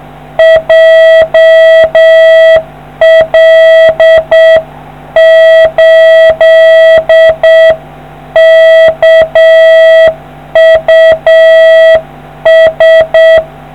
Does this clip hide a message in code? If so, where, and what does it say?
Yes: Morse code from 0.4 to 13.4 s, heard as a keyed tone, about 7 wpm, 650 Hz. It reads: JL8KUS